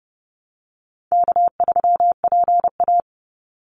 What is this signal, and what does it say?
Morse code, heard as a keyed tone, about 30 wpm, 700 Hz. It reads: K3PA
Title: Easy Call Signs Repeated 2x - 30wpm